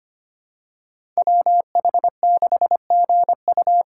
Morse code, heard as a keyed tone, about 25 wpm, 700 Hz. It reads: WH6GU
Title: USA Call Signs - 3x Character Spacing - 25wpm